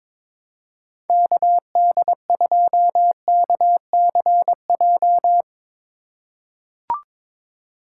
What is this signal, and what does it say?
Morse code, heard as a keyed tone, about 22 wpm, 700 Hz. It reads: KD2KCJ
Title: USA Call Signs - 6 Characters - 2x Character Spacing - 22wpm